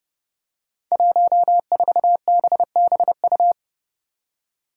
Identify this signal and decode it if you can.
Morse code, heard as a keyed tone, about 32 wpm, 700 Hz. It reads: 14BBU
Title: Five Challenging Letter-Number Combinations 30wpm